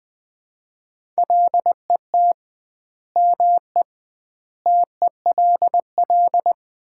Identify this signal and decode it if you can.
Morse code, heard as a keyed tone, about 20 wpm, 700 Hz. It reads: LET ME TELL
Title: 2x Word-Spacing - Sets of 4 Words - 20wpm